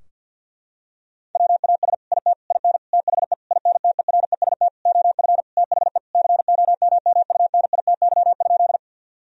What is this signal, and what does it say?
Morse code, heard as a keyed tone, about 50 wpm, 700 Hz. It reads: WAS IT IN THE INTEREST OF THE COMMUNITY?
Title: Sentences from Top 1000 Words - Review Part 4 - 50wpm